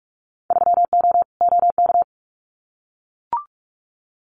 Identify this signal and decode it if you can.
Morse code, heard as a keyed tone, about 44 wpm, 690 Hz. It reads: 3O OK
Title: Winter Field Day Exchanges Encore 45wpm